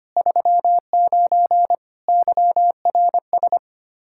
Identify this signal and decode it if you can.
Morse code, heard as a keyed tone, about 25 wpm, 700 Hz. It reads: 39 YRS
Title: Practice QSO - Exchange 3 - More Info - 25wpm